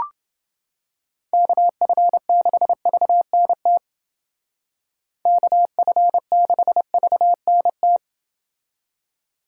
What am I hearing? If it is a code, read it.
Morse code, heard as a keyed tone, about 29 wpm, 700 Hz. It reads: KF6VNT KF6VNT